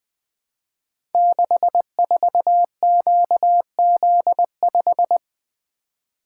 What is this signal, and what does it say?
Morse code, heard as a keyed tone, about 20 wpm, 700 Hz. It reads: 64QZ5
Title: Code Groups - 5 Letter:Numbers - 20wpm